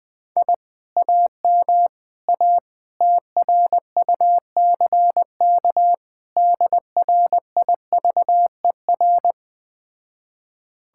Morse code, heard as a keyed tone, about 20 wpm, 700 Hz. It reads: I AM A TRUCK DRIVER
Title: Speed-Racing Common QSO Phrases - 17wpm